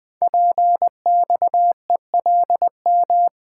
Morse code, heard as a keyed tone, about 20 wpm, 700 Hz. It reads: PXELM
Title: Code Groups - 5 Letters - 20wpm